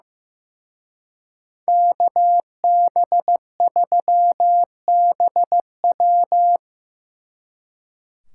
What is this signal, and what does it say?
Morse code, heard as a keyed tone, about 17 wpm, 700 Hz. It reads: KB3BW
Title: Speed Racing - USA Call Signs 15wpm